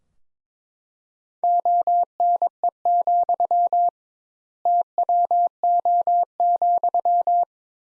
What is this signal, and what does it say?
Morse code, heard as a keyed tone, about 22 wpm, 700 Hz. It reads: ONE, TWO,
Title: Sentences Top 200 Words - 22wpm